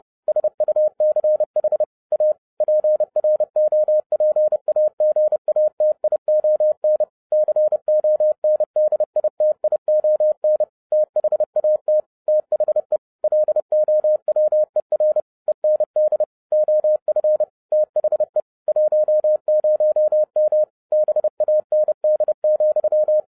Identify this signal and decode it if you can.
Morse code, heard as a keyed tone, about 30 wpm, 610 Hz. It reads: SUCH A PROPAGATION CONDITION THAT THE LOWER END OF THE 10M BAND,
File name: 2012-01-mp3 / n12-I DIDNT MISS THE SOLAR CYCLE 24-30wpm